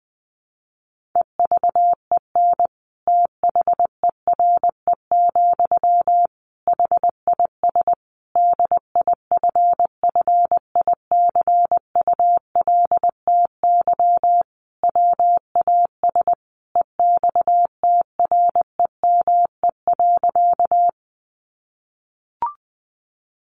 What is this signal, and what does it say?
Morse code, heard as a keyed tone, about 19 wpm, 700 Hz. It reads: EVEN THERE, HIS DIFFICULTY WAS EXTREME.
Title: Sentences from Top 1500 Words 20wpm